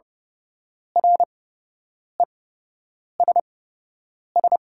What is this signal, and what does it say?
Morse code, heard as a keyed tone, about 30 wpm, spaced as for 8 wpm, 700 Hz. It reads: RESS
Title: 8x Character Spacing - Top 300 Words - 30wpm